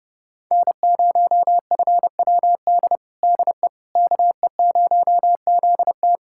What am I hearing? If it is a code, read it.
Morse code, heard as a keyed tone, about 32 wpm, 700 Hz. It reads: N0FWD DE KE0ZT
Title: USA Callsign DE Callsign - 5 Characters - 30wpm